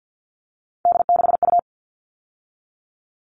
Morse code, heard as a keyed tone, about 50 wpm, 700 Hz. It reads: D6U
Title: All Cut Numbers - 3-Digit 50wpm